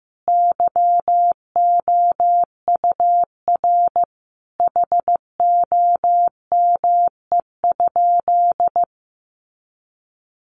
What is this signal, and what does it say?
Morse code, heard as a keyed tone, about 15 wpm, 700 Hz. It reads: YOUR HOME?
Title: Sentences from Top 1000 Words 15wpm